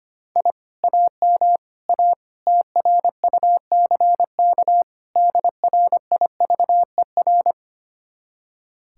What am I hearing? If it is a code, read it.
Morse code, heard as a keyed tone, about 32 wpm, 700 Hz. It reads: I AM A TRUCK DRIVER